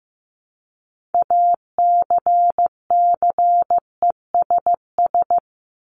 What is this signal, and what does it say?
Morse code, heard as a keyed tone, about 15 wpm, 700 Hz. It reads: ACCESS
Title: Top 501-750 Words 15wpm